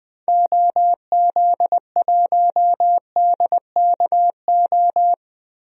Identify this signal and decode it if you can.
Morse code, heard as a keyed tone, about 21 wpm, 700 Hz. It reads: OZ1DKO
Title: DXCC-100 Call Signs - 6 Characters - 20wpm